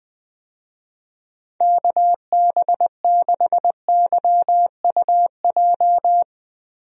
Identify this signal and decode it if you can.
Morse code, heard as a keyed tone, about 20 wpm, 700 Hz. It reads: KB6YUJ
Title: USA Call Signs - 6 Characters - Repeated 2x - 20wpm